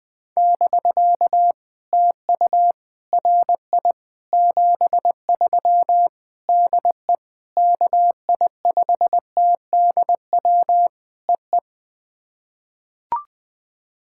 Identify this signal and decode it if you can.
Morse code, heard as a keyed tone, about 21 wpm, 700 Hz. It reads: <BK> TU RI 73 DE KI5TDW EE
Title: POTA QSO Practice - 20wpm